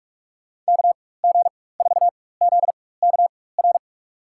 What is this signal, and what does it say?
Morse code, heard as a keyed tone, about 46 wpm, spaced as for 21 wpm, 700 Hz. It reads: KG4ZKR
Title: USA Call Signs - 6 Characters - 4x Character Spacing - 45wpm